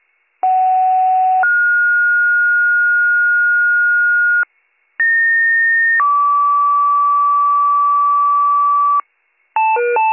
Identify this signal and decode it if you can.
Two-tone page, heard as a one-second tone, then a three-second tone, about 750 then 1430 Hz, about 1850 then 1150 Hz, with noise. It is heard from 0.4 to 9.0 s.